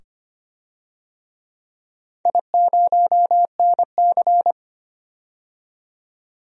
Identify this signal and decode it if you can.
Morse code, heard as a keyed tone, about 25 wpm, 700 Hz. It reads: I0NC
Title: DXCC-100 Call Signs - 4 Characters - 25wpm